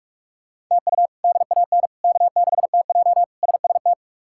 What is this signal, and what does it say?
Morse code, heard as a keyed tone, about 41 wpm, 700 Hz. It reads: TU DAN K6TJ SST
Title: K1USN SST QSO Practice - 45wpm